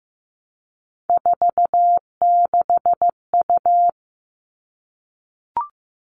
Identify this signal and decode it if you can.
Morse code, heard as a keyed tone, about 15 wpm, 700 Hz. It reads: 46U
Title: All Cut Numbers - 3-Digit 15wpm